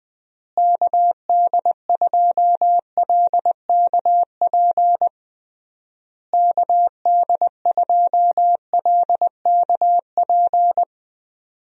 Morse code, heard as a keyed tone, about 19 wpm, 700 Hz. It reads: KD2LKP KD2LKP
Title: USA Call Signs - 6 Characters - Repeated 3x - 20wpm